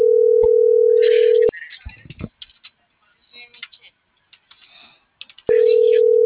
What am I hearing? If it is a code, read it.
Ringback tone (North America), 440 + 480 Hz.